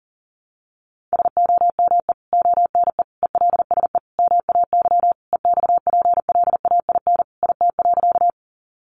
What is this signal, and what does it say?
Morse code, heard as a keyed tone, about 40 wpm, 700 Hz. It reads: SOME ONE ELSE MAY EXPLAIN IT.
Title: Sentences from Top 1100 Words 40wpm